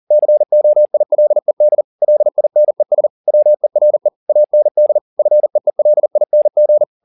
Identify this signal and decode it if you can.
Morse code, heard as a keyed tone, about 40 wpm, 600 Hz. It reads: COILED LINES WERE AND FEELING
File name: the old man and the sea 1 40wpm